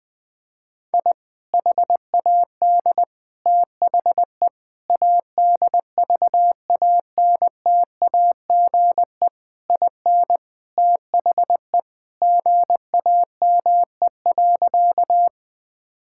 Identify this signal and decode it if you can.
Morse code, heard as a keyed tone, about 20 wpm, 700 Hz. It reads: I HAD THE ADVANTAGE IN THE GAME.
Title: Sentences from Top 1000 Words - Review Part 6 - 20wpm